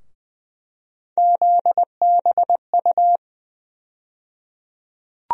Morse code, heard as a keyed tone, about 20 wpm, 700 Hz. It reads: ZBU
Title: USA Call Sign Postfixes - 3 Letters 20wpm